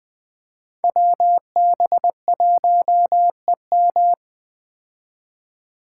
Morse code, heard as a keyed tone, about 20 wpm, 700 Hz. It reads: WB1EM
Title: Code Groups - 5 Letter:Numbers - 20wpm